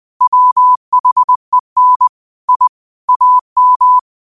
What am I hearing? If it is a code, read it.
Morse code, heard as a keyed tone, about 20 wpm, 1000 Hz. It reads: WHEN I AM